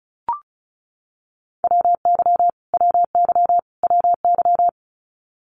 Morse code, heard as a keyed tone, about 34 wpm, 700 Hz. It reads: WY WY WY
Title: US States Repeated 3x 35wpm